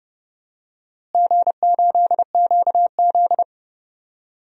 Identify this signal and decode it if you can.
Morse code, heard as a keyed tone, about 32 wpm, 700 Hz. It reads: G8QZ
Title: DXCC-100 Call Signs - 4 Characters - Repeated 3x - 30wpm